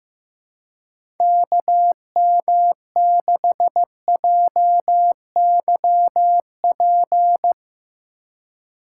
Morse code, heard as a keyed tone, about 15 wpm, 700 Hz. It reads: KM6JYP
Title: USA Call Signs - 4x Character Spacing - 15wpm